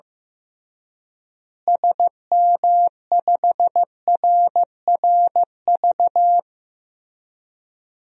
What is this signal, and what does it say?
Morse code, heard as a keyed tone, about 15 wpm, 700 Hz. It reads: SM5RRV
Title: DXCC-100 Call Signs - 6 Characters - Repeated 3x - 15wpm